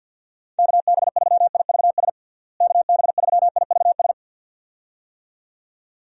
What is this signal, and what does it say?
Morse code, heard as a keyed tone, about 51 wpm, 700 Hz. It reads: KB3IVS KB3IVS
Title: USA Call Signs - 6 Characters - Repeated 2x - 50wpm